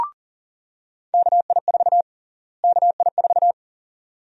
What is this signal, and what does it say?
Morse code, heard as a keyed tone, about 40 wpm, 700 Hz. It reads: KI4 KI4
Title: USA Call Sign Prefixes - Repeated 2x - 40wpm